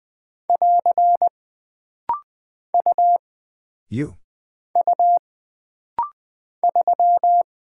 Morse code, heard as a keyed tone, about 20 wpm, 700 Hz. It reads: <AR> U U 3
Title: Single Character (Letter:Number:Punctuation:Prosign) - Rapid-Fire - 20wpm